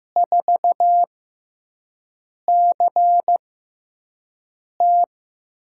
Morse code, heard as a keyed tone, about 15 wpm, spaced as for 5 wpm, 700 Hz. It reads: 4CT
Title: USA Call Signs - 6x Character Spacing - 15wpm